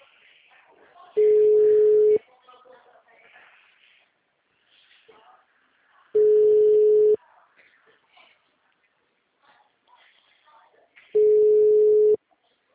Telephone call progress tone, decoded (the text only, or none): ringback tone